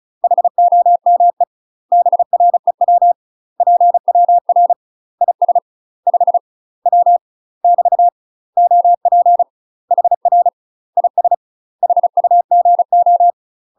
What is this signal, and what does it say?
Morse code, heard as a keyed tone, about 36 wpm, 700 Hz. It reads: HOME BREW PWR IS 5 W <BT> OP HR IS HUGO
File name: UK MASTER QSOs TRAINING / 35WPM / UK MASTER QSOs 0031